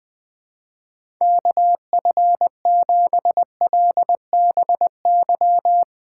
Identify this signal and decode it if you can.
Morse code, heard as a keyed tone, about 20 wpm, 700 Hz. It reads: KF7LBY
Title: USA Call Signs - 6 Characters - 2x Character Spacing - 20wpm